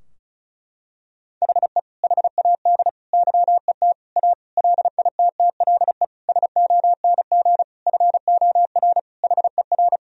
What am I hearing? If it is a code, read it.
Morse code, heard as a keyed tone, about 34 wpm, 700 Hz. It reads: HE HAD YET A LITTLE SONG FOR HER
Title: Sentences from Top 1000 Words - Review Part 2 - 35wpm